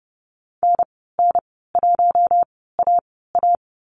Morse code, heard as a keyed tone, about 32 wpm, spaced as for 17 wpm, 700 Hz. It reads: NN1AA
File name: Lesson 15g - Number 5 - Call Signs 30wpm